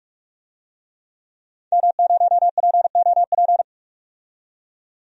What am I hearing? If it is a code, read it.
Morse code, heard as a keyed tone, about 42 wpm, 700 Hz. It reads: M0POP
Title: DXCC-100 Call Signs - 5 Characters - 45wpm